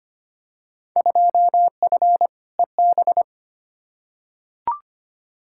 Morse code, heard as a keyed tone, about 25 wpm, 700 Hz. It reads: 2F EB
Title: Field Day Exchanges Encore - 25wpm